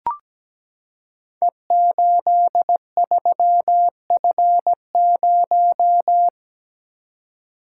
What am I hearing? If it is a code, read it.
Morse code, heard as a keyed tone, about 17 wpm, 700 Hz. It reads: E83F0
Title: Code Groups - 5 Letter:Numbers - 17wpm